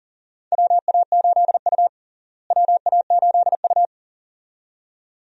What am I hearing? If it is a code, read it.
Morse code, heard as a keyed tone, about 40 wpm, 700 Hz. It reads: WA8U WA8U
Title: USA Call Signs - 4 Characters - Repeated 2x - 40wpm